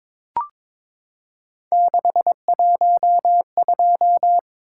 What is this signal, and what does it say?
Morse code, heard as a keyed tone, about 22 wpm, 700 Hz. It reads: 612
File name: practice / 22wpm / Three Letter-Number Combinations - 22wpm